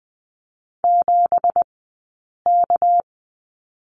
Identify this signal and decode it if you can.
Morse code, heard as a keyed tone, about 20 wpm, spaced as for 8 wpm, 690 Hz. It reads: 7K
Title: 197 Lesson 32d - Letter k - Two Characters 20wpm